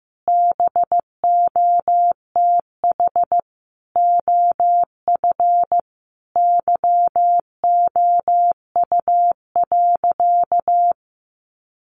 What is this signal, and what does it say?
Morse code, heard as a keyed tone, about 15 wpm, 700 Hz. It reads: BOTH OF YOU.
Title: Sentences from Top 1100 Words 15wpm